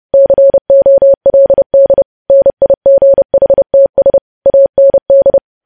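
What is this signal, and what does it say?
Morse code, heard as a keyed tone, about 30 wpm, 550 Hz. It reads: COLD NIGHTS AND
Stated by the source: Morse code news headlines sent at 30, preceded by vvv vvv.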